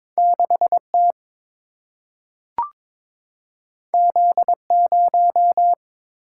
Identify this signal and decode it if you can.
Morse code, heard as a keyed tone, about 22 wpm, 700 Hz. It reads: 6T Z0
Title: Single-Letter Single-Number 22wpm